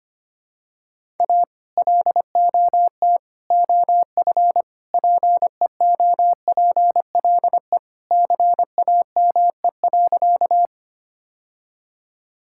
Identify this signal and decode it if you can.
Morse code, heard as a keyed tone, about 25 wpm, 700 Hz. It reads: A LOT OF PEOPLE CAME.
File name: practice / 25wpm / Sentences with 3-Word Phrases - 25wpm